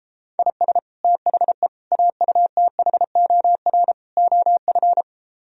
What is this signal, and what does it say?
Morse code, heard as a keyed tone, about 32 wpm, 700 Hz. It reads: IS THE AUTHOR OF
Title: Speed-Racing - Sets of 4 Words 28wpm